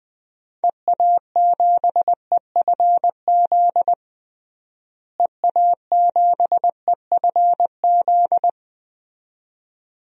Morse code, heard as a keyed tone, about 21 wpm, 700 Hz. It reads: EA7EFZ EA7EFZ
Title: DXCC-100 Call Signs - 6 Characters - Repeated 2x - 20wpm